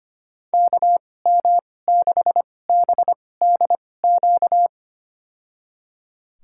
Morse code, heard as a keyed tone, about 25 wpm, spaced as for 18 wpm, 700 Hz. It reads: KM6BDQ
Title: USA Call Signs - 6 Characters - 2x Character Spacing - 25wpm